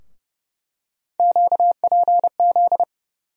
Morse code, heard as a keyed tone, about 30 wpm, 700 Hz. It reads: QPZ
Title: USA Call Sign Postfixes - 3 Letters 30wpm